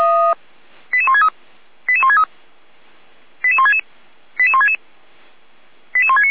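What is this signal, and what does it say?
Five-tone selective call (ZVEI), heard as quick runs of five tones, tones of about 70 ms. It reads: 88163 88163 88166 88166 88169
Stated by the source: ZVEI five-tone selective call received by SDR.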